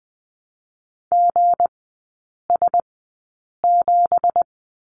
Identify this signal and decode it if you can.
Morse code, heard as a keyed tone, about 20 wpm, spaced as for 8 wpm, 700 Hz. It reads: GS7
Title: 166 Lesson 27e - Number 7 - Three Characters 20wpm